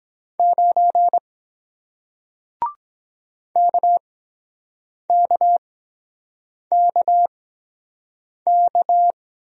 Morse code, heard as a keyed tone, about 23 wpm, spaced as for 7 wpm, 700 Hz. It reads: 9 KKKK